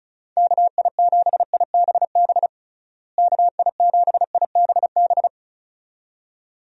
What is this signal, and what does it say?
Morse code, heard as a keyed tone, about 34 wpm, 700 Hz. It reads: KI7IBB KI7IBB
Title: USA Call Signs - 6 Characters - Repeated 2x - 35wpm